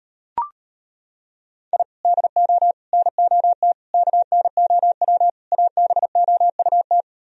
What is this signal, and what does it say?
Morse code, heard as a keyed tone, about 33 wpm, 700 Hz. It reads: I DO NOT KNOW ABOUT